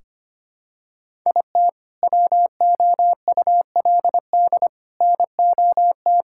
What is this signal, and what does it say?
Morse code, heard as a keyed tone, about 25 wpm, 700 Hz. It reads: IT WOULD NOT
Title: Sentences from Top 1600 Words 25wpm